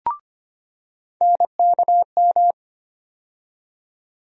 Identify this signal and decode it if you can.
Morse code, heard as a keyed tone, about 25 wpm, 700 Hz. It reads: NKM
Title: USA Call Sign Postfixes - 3 Letters 25wpm